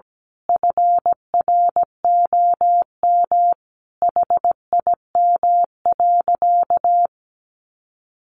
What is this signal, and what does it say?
Morse code, heard as a keyed tone, about 17 wpm, 700 Hz. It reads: FROM HIM.
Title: Sentences from Top 1500 Words 17wpm